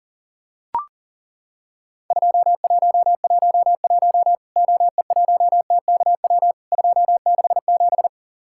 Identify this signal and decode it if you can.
Morse code, heard as a keyed tone, about 40 wpm, 700 Hz. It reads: 2111 OE1TKW 267
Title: Worked All Europe DX Contest - QTC Exchange - 40wpm